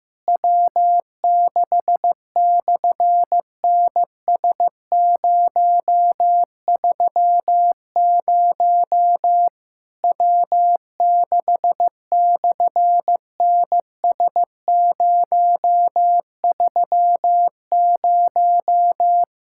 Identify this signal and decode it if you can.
Morse code, heard as a keyed tone, about 15 wpm, 700 Hz. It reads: W6/NS030 W6/NS030
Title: SOTA QSO Practice 15wpm